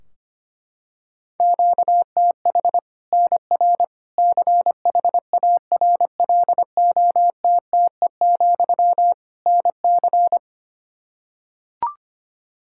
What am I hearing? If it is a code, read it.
Morse code, heard as a keyed tone, about 25 wpm, 700 Hz. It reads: QTH NR CHARLOTTE, NC
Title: Practice QSO - Exchange 1 - Beginning - 25wpm